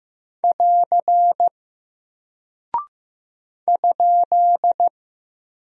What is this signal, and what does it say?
Morse code, heard as a keyed tone, about 15 wpm, 700 Hz. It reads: <AR> ?
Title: Single Character (Letter:Number:Punctuation:Prosign) Alt 15wpm